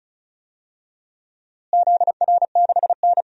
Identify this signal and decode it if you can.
Morse code, heard as a keyed tone, about 36 wpm, 700 Hz. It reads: ZR6N
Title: DXCC-100 Call Signs - 4 Characters - Repeated 3x - 35wpm